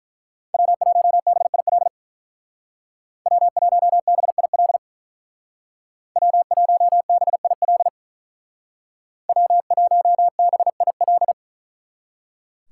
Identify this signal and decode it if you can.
Morse code, heard as a keyed tone, about 43 wpm, 700 Hz. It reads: W1BIL W1BIL W1BIL W1BIL
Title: Speed Racing - USA 1x3 Call Signs 35wpm